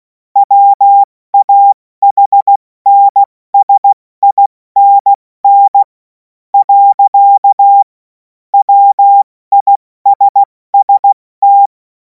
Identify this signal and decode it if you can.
Morse code, heard as a keyed tone, about 16 wpm, spaced as for 13 wpm, 800 Hz. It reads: WAHNSINN. WISST
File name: Morsekurs / Witze / Witz 11. August 2026 70 F80